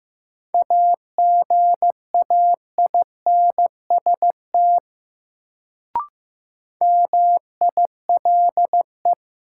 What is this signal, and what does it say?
Morse code, heard as a keyed tone, about 15 wpm, 700 Hz. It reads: AGAINST MILE
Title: Sentences from Top 400 Words - New Word Drill - 15wpm